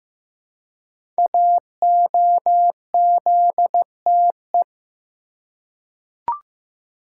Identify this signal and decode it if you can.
Morse code, heard as a keyed tone, about 15 wpm, 700 Hz. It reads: AOZTE
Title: Code Groups - 5 Letters - 15wpm